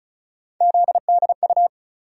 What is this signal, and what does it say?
Morse code, heard as a keyed tone, about 34 wpm, 700 Hz. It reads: ZDU
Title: USA Call Sign Postfixes - 3 Letters 35wpm